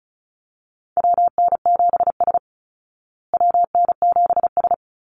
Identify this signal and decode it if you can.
Morse code, heard as a keyed tone, about 34 wpm, 700 Hz. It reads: WN7S WN7S
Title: Call Signs Repeated 2x - 35wpm